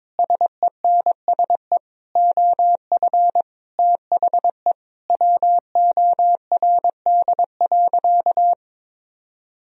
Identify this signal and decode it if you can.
Morse code, heard as a keyed tone, about 21 wpm, 700 Hz. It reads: SENSE OF THE WORD.
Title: Sentences from Top 1000 Words - Review Part 4 - 22wpm